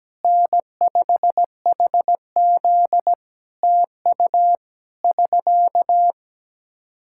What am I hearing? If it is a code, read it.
Morse code, heard as a keyed tone, about 17 wpm, 700 Hz. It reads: N5HZ TU <SK>
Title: Practice QSO - Exchange 4 - Ending - 17wpm